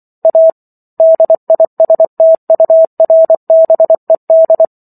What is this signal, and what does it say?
Morse code, heard as a keyed tone, about 24 wpm, 650 Hz. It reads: A DISTURBED